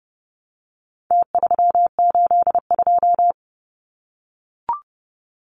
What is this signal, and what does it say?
Morse code, heard as a keyed tone, about 30 wpm, 700 Hz. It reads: T382
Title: Numbers - Contest Style - 4-Digit 30wpm